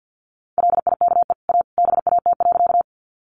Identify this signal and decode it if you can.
Morse code, heard as a keyed tone, about 50 wpm, 700 Hz. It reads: LIKE A BAT.
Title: Sentences from Top 1500 Words 50wpm